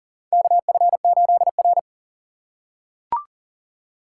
Morse code, heard as a keyed tone, about 39 wpm, 700 Hz. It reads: KF8R
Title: USA Call Signs - 6x Character Spacing - 40wpm